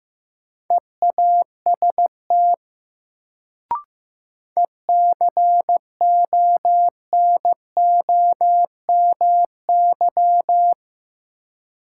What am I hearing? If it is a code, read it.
Morse code, heard as a keyed tone, about 15 wpm, 700 Hz. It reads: EAST ECONOMY
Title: Sentences from Top 1000 Words - Review Part 3 - New Word Drill - 15wpm